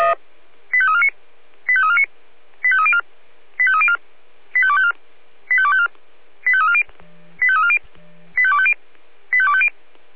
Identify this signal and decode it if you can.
Five-tone selective call (ZVEI), heard as quick runs of five tones, tones of about 70 ms. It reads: 85349 85349 85304 85304 85254 85254 85340 85340 85250 85250